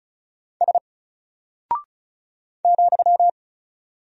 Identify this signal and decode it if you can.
Morse code, heard as a keyed tone, about 36 wpm, 700 Hz. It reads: S ,
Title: Single Character (Letter:Number:Punctuation:Prosign) 35wpm